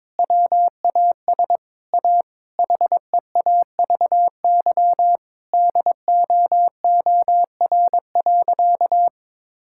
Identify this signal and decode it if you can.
Morse code, heard as a keyed tone, about 22 wpm, 700 Hz. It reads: WAS A HEAVY DOOR.
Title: Sentences from Top 1600 Words 22wpm